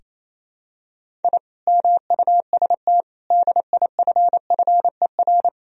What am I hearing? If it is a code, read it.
Morse code, heard as a keyed tone, about 28 wpm, 700 Hz. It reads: I MUST DIFFER
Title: Sentences from Top 1000 Words - Review Part 1 - 28wpm